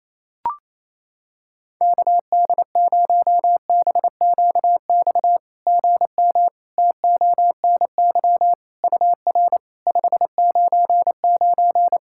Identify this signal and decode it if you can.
Morse code, heard as a keyed tone, about 27 wpm, 690 Hz. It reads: KD0BQX GM TONY UR 599
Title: SOTA QSO Practice 28wpm